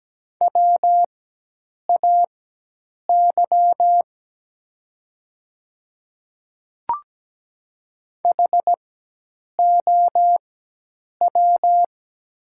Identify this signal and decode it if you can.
Morse code, heard as a keyed tone, about 17 wpm, spaced as for 8 wpm, 700 Hz. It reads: WAY HOW
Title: 4x Character Spacing - Top 100 Words - 17wpm